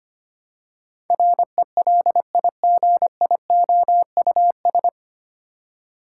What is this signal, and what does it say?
Morse code, heard as a keyed tone, about 25 wpm, 700 Hz. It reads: RELIGIOUS
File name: Nine Letter Words - 25wpm